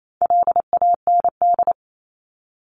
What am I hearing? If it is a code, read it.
Morse code, heard as a keyed tone, about 28 wpm, 700 Hz. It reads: LAND
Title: Top 500 Words 28wpm